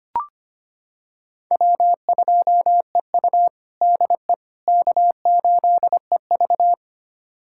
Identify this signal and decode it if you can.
Morse code, heard as a keyed tone, about 25 wpm, 700 Hz. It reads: W2EU DE K8EV
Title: USA Callsign DE Callsign - 4 Characters - 25wpm